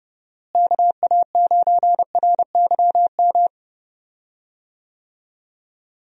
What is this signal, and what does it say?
Morse code, heard as a keyed tone, about 30 wpm, 700 Hz. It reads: KA9RYM
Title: USA Call Signs - 6 Characters - Repeated 3x - 30wpm